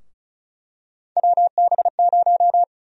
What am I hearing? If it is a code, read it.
Morse code, heard as a keyed tone, about 34 wpm, 700 Hz. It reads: WB0
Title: USA Call Sign Prefixes - 35wpm